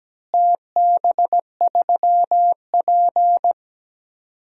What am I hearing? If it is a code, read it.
Morse code, heard as a keyed tone, about 17 wpm, 700 Hz. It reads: TB3P
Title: DXCC-100 Call Signs - 4 Characters - Repeated 3x - 17wpm